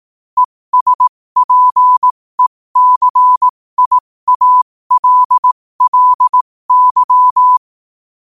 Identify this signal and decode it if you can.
Morse code, heard as a keyed tone, about 18 wpm, spaced as for 15 wpm, 1000 Hz. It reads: ESPECIALLY
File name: qotdfiles / 15wpm